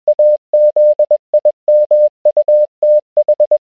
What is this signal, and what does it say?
Morse code, heard as a keyed tone, about 21 wpm, 600 Hz. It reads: AZIMUTH